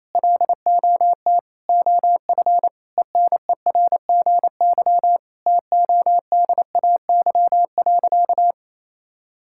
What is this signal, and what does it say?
Morse code, heard as a keyed tone, about 28 wpm, 700 Hz. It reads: LOT OF ENERGY TODAY.
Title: Sentences from Top 1000 Words - Review Part 6 - 28wpm